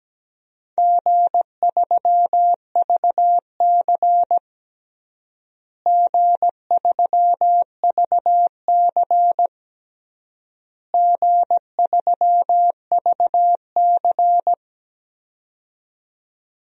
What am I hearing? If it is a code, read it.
Morse code, heard as a keyed tone, about 17 wpm, 700 Hz. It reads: G3VC G3VC G3VC
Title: DXCC-100 Call Signs - 4 Characters - Repeated 3x - 17wpm